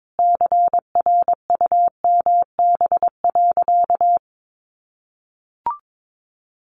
Morse code, heard as a keyed tone, about 22 wpm, 700 Hz. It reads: CRUMB.
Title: Easy Phrases 22wpm